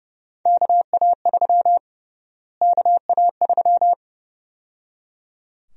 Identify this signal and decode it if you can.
Morse code, heard as a keyed tone, about 30 wpm, 700 Hz. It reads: KA3 KA3
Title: USA Call Sign Prefixes - Repeated 2x - 30wpm